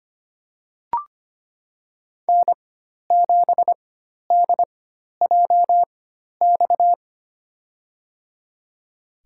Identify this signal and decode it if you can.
Morse code, heard as a keyed tone, about 25 wpm, spaced as for 12 wpm, 700 Hz. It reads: N7DJX
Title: USA Call Signs - 5 Characters - 4x Character Spacing - 25wpm